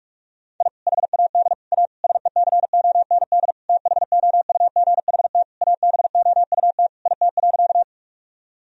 Morse code, heard as a keyed tone, about 44 wpm, 700 Hz. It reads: I HAD A SECOND THOUGHT ABOUT IT.
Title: Sentences from Top 1000 Words - Review Part 1 - 45wpm